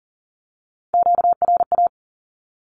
Morse code, heard as a keyed tone, about 40 wpm, 700 Hz. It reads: QRA
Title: Q-Codes 40wpm